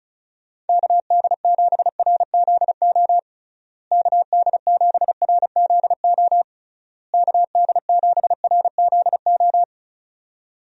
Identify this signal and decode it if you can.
Morse code, heard as a keyed tone, about 36 wpm, 700 Hz. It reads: KD7RZO KD7RZO KD7RZO
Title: USA Call Signs - 6 Characters - Repeated 3x - 35wpm